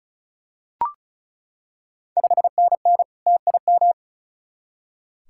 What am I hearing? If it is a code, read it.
Morse code, heard as a keyed tone, about 36 wpm, 690 Hz. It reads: 5NN TIM
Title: Ontario QSO Party Exchange 35wpm